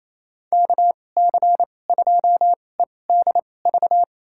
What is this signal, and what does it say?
Morse code, heard as a keyed tone, about 28 wpm, spaced as for 20 wpm, 700 Hz. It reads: KC2EDV
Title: USA Call Signs - 6 Characters - 2x Character Spacing - 28wpm